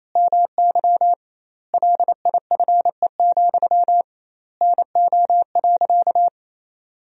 Morse code, heard as a keyed tone, about 28 wpm, 700 Hz. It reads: MY LIFE, NO.
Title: 2x Word-Spacing - Sentences from Top 300 Words - 28wpm